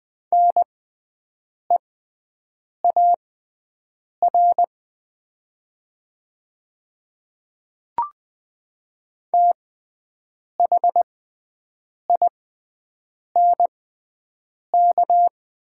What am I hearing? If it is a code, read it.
Morse code, heard as a keyed tone, about 20 wpm, spaced as for 7 wpm, 700 Hz. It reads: NEAR THINK
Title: 6x Character Spacing - Top 200 Words - 20wpm